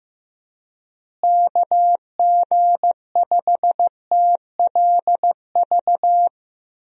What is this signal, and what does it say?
Morse code, heard as a keyed tone, about 15 wpm, 700 Hz. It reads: KG5TLV
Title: USA Call Signs - 6 Characters - Repeated 3x - 15wpm